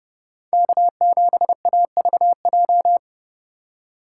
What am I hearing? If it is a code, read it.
Morse code, heard as a keyed tone, about 30 wpm, 700 Hz. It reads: K7AVJ